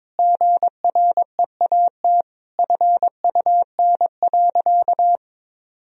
Morse code, heard as a keyed tone, about 22 wpm, 700 Hz. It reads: GREAT FUN.
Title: Sentences from Top 1000 Words - Review Part 4 - 22wpm